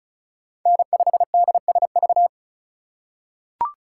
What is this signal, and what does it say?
Morse code, heard as a keyed tone, about 34 wpm, 700 Hz. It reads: N5DSV
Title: USA Call Signs - 2x Character Spacing - 35wpm